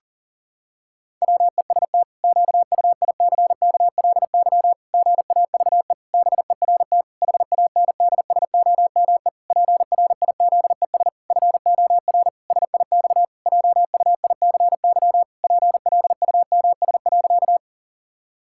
Morse code, heard as a keyed tone, about 40 wpm, 700 Hz. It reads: WEST QUICKLY GAVE BERT HANDSOME PRIZES FOR SIX JUICY PLUMS.
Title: Pangram Challenge v2 - 40wpm